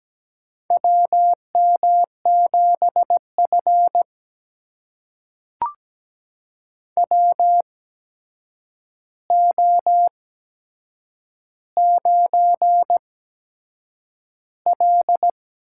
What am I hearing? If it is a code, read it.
Morse code, heard as a keyed tone, about 17 wpm, spaced as for 7 wpm, 700 Hz. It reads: WM7F WO9L
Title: USA Call Signs - 4 Characters - 8x Character Spacing - 17wpm